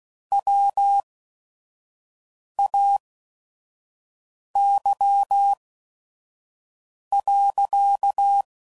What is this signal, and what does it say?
Morse code, heard as a keyed tone, about 16 wpm, spaced as for 5 wpm, 800 Hz. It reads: WAY.
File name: CW Code Groups @5WPM / Technical Subjects / Gates Intro